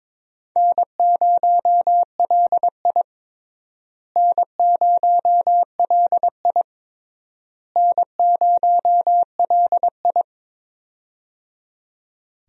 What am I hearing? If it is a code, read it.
Morse code, heard as a keyed tone, about 22 wpm, 700 Hz. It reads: N0LI N0LI N0LI
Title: USA Call Signs - 4 Characters - Repeated 3x - 22wpm